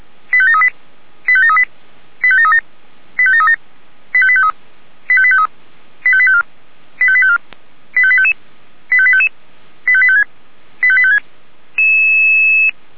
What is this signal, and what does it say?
Five-tone selective call (ZVEI), heard as quick runs of five tones, tones of about 70 ms. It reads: 85738 85738 85737 85737 85853 85853 85854 85854 85855 85855 85856 85856